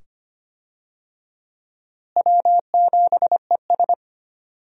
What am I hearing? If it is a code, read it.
Morse code, heard as a keyed tone, about 25 wpm, 700 Hz. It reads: W7ES
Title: USA Call Signs - 4 Characters - Repeated 2x - 25wpm